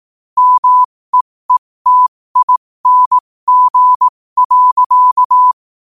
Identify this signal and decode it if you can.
Morse code, heard as a keyed tone, about 18 wpm, spaced as for 15 wpm, 1000 Hz. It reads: MEETING.
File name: qotdfiles / 15wpm